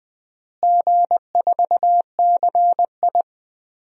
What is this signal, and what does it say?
Morse code, heard as a keyed tone, about 20 wpm, 700 Hz. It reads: G4CI
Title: DXCC-100 Call Signs - 4 Characters - Repeated 2x - 20wpm